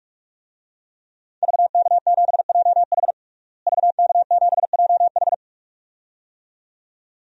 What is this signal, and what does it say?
Morse code, heard as a keyed tone, about 46 wpm, 700 Hz. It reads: VK7JH VK7JH
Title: DXCC-100 Call Signs - 5 Characters - Repeated 2x - 45wpm